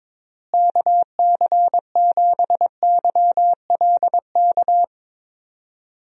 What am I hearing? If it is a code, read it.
Morse code, heard as a keyed tone, about 22 wpm, 700 Hz. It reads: KC7YLK
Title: USA Call Signs - 6 Characters - Repeated 2x - 22wpm